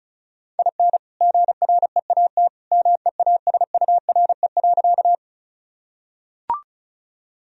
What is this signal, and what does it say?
Morse code, heard as a keyed tone, about 34 wpm, 700 Hz. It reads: IN GREAT MEASURE.
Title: Sentences from Top 1000 Words - Review Part 2 - 35wpm